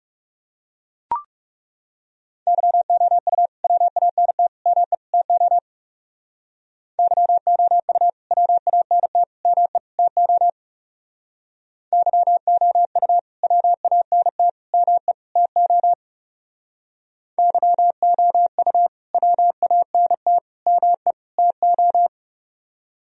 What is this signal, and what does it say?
Morse code, heard as a keyed tone, about 39 wpm, 700 Hz. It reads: YOU WANT ME TO YOU WANT ME TO YOU WANT ME TO YOU WANT ME TO